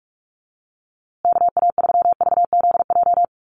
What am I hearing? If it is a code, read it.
Morse code, heard as a keyed tone, about 44 wpm, 700 Hz. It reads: KA3VZJ
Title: USA Call Signs Repeated 3x 45wpm